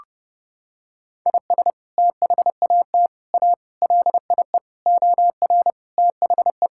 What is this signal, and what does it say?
Morse code, heard as a keyed tone, about 32 wpm, 700 Hz. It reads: IS THAT A LIE OR THE
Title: Sentences from Top 1600 Words 30wpm